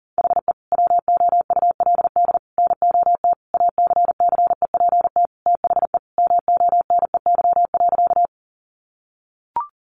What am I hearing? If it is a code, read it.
Morse code, heard as a keyed tone, about 40 wpm, 700 Hz. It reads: HE WOULD NOT ACCEPT THE MONEY.
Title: Sentences from Top 1300 Words 40wpm